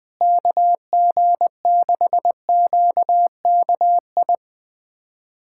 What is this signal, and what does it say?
Morse code, heard as a keyed tone, about 20 wpm, 700 Hz. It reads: KG6QKI
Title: USA Callsign DE Callsign 20wpm